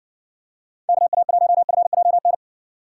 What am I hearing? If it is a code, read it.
Morse code, heard as a keyed tone, about 60 wpm, 700 Hz. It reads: KA1FJN